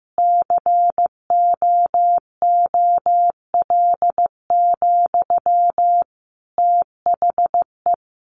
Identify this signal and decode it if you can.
Morse code, heard as a keyed tone, about 15 wpm, 700 Hz. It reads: COOL, THE
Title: Sentences from Top 1200 Words 15wpm